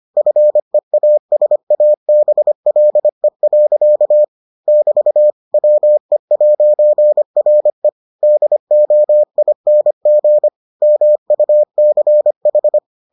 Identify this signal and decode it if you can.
Morse code, heard as a keyed tone, about 25 wpm, 600 Hz. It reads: FEASABLE. <BT> WE'RE DOING MUCH